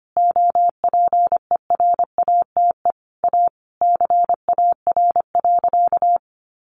Morse code, heard as a keyed tone, about 25 wpm, 700 Hz. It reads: OPERATE A CAR.
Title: Sentences from Top 1500 Words 25wpm